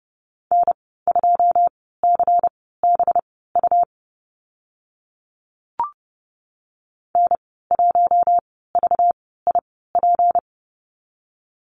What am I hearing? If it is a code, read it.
Morse code, heard as a keyed tone, about 30 wpm, spaced as for 17 wpm, 700 Hz. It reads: N2CBU N1VIP
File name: Lesson 37g - Letter x - Call Signs 30wpm